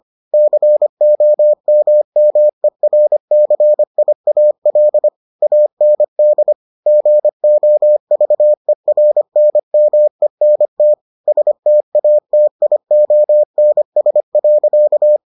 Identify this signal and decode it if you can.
Morse code, heard as a keyed tone, about 25 wpm, 600 Hz. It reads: COMMERCIAL AND GOVERNMENT STATIONS.